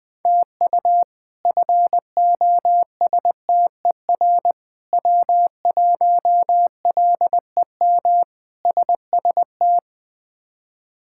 Morse code, heard as a keyed tone, about 21 wpm, 700 Hz. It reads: TU FOSTER W1LEM SST
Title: K1USN SST QSO Practice - 20wpm